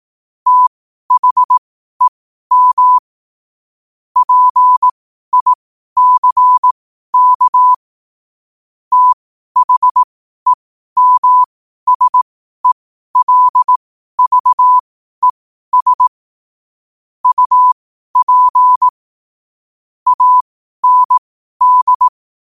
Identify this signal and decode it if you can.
Morse code, heard as a keyed tone, about 18 wpm, spaced as for 12 wpm, 1000 Hz. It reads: THEM PICK THEMSELVES UP AND
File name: qotdfiles / 12wpm